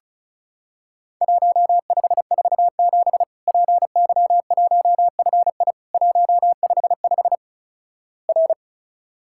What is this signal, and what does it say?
Morse code, heard as a keyed tone, about 34 wpm, 690 Hz. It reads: 1547 PY1FI 155 R
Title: Worked All Europe DX Contest - QTC Exchange - 35wpm